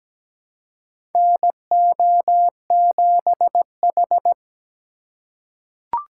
Morse code, heard as a keyed tone, about 17 wpm, 700 Hz. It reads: NO7H
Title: USA Call Signs - 4 Characters - 4x Character Spacing - 17wpm